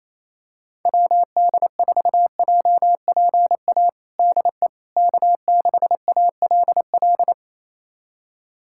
Morse code, heard as a keyed tone, about 28 wpm, 700 Hz. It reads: WD4JPA DE K6ALL
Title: USA Callsign DE Callsign 28wpm